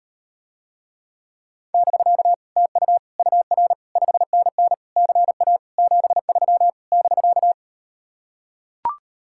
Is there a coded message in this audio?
Yes: Morse code from 1.7 to 7.5 s, heard as a keyed tone, about 39 wpm, 700 Hz. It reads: <BK> TU UR 5NN CA 73 <BK>